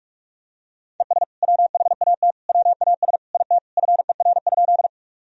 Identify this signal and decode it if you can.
Morse code, heard as a keyed tone, about 46 wpm, 700 Hz. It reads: ES WHAT WAS IT FER?
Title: Sentences from Top 100 Words and Abbreviations 45wpm